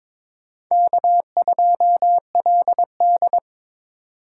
Morse code, heard as a keyed tone, about 22 wpm, 700 Hz. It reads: K2LD
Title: USA Call Signs - 2x Character Spacing - 22wpm